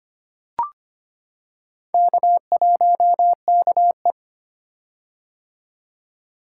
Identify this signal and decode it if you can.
Morse code, heard as a keyed tone, about 25 wpm, 700 Hz. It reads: K1KE
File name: practice / 25wpm / Four Letter-Number Combinations - 25wpm